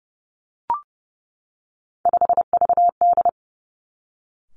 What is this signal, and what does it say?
Morse code, heard as a keyed tone, about 30 wpm, 700 Hz. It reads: 5VD
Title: Three Challenging Letter-Number Combinations 30wpm